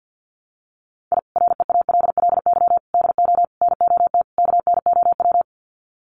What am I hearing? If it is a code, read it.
Morse code, heard as a keyed tone, about 50 wpm, 700 Hz. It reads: I REALLY DO NOT KNOW
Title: Sets of 5 Words Encore 50wpm